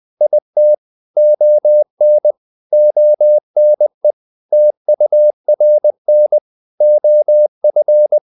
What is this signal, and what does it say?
Morse code, heard as a keyed tone, about 20 wpm, 600 Hz. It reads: IT ON ONE TURN OF
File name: CW / Practice / If20